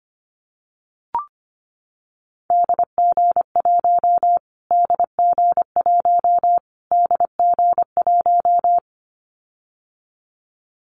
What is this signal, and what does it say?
Morse code, heard as a keyed tone, about 25 wpm, 700 Hz. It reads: DG1 DG1 DG1
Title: Callsign Prefixes for Top 25 DXCC - Repeated 3x - 25wpm